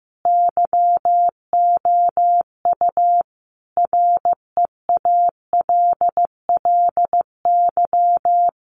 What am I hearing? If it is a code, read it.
Morse code, heard as a keyed tone, about 15 wpm, 700 Hz. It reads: YOU REALLY
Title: Sentences from Top 600 Words 15wpm